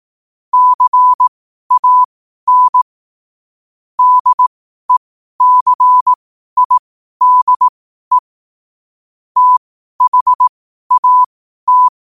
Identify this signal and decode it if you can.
Morse code, heard as a keyed tone, about 18 wpm, spaced as for 12 wpm, 1000 Hz. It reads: CAN DECIDE THAT
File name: qotdfiles / 12wpm